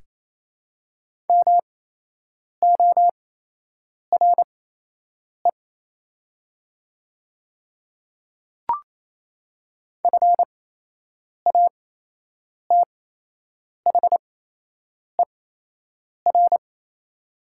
Morse code, heard as a keyed tone, about 28 wpm, spaced as for 8 wpm, 690 Hz. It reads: MORE FATHER
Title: 8x Character Spacing - Top 300 Words - 28wpm